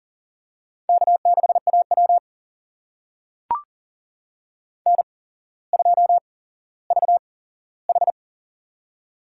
Morse code, heard as a keyed tone, about 40 wpm, spaced as for 17 wpm, 690 Hz. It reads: K6AW N2VH
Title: USA Call Signs - 4 Characters - 8x Character Spacing - 40wpm